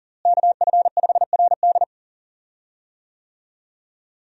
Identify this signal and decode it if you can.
Morse code, heard as a keyed tone, about 37 wpm, 700 Hz. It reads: KF5RD
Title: USA Call Signs - 5 Characters - Repeated 2x - 40wpm